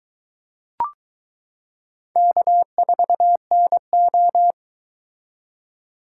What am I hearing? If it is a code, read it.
Morse code, heard as a keyed tone, about 22 wpm, 700 Hz. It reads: K4NO